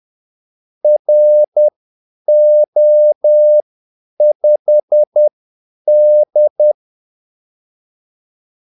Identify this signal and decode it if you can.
Morse code, heard as a keyed tone, about 10 wpm, spaced as for 8 wpm, 600 Hz. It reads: RO5D
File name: Callsigns-10-8wpm